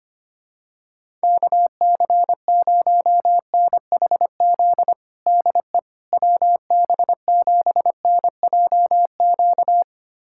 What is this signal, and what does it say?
Morse code, heard as a keyed tone, about 25 wpm, 700 Hz. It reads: KC0NHZ DE WB7NJQ
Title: USA Callsign DE Callsign - 6 Characters - 25wpm